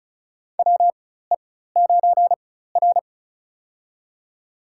Morse code, heard as a keyed tone, about 36 wpm, spaced as for 16 wpm, 700 Hz. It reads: WE9R
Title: USA Call Signs - 4 Characters - 4x Character Spacing - 35wpm